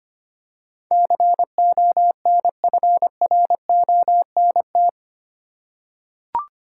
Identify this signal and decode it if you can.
Morse code, heard as a keyed tone, about 25 wpm, 700 Hz. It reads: CONFRONT
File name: Eight Letter Words - 25wpm